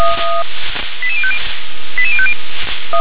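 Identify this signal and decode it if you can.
Five-tone selective call (ZVEI), heard as quick runs of five tones, tones of about 70 ms. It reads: 88050 88050
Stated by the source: ZVEI five-tone selective call 88051, received by SDR.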